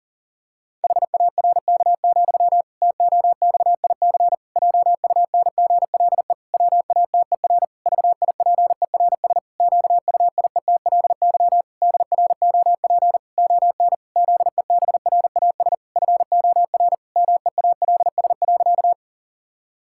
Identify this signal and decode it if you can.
Morse code, heard as a keyed tone, about 40 wpm, 700 Hz. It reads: HARK, TOXIC JUNGLE WATER VIPERS QUIETLY DROP ON ZEBRAS FOR MEALS.
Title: Pangram Challenge v2 - 40wpm